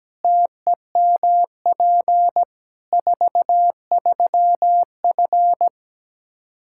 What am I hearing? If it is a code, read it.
Morse code, heard as a keyed tone, about 17 wpm, 700 Hz. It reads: TEMP 43F
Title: Practice QSO - Exchange 2 - Info - 17wpm